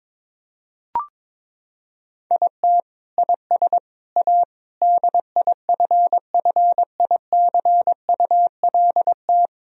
Morse code, heard as a keyed tone, about 23 wpm, 700 Hz. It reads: IT IS A DIFFICULT
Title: Sentences from Top 1000 Words - Review Part 3 - 22wpm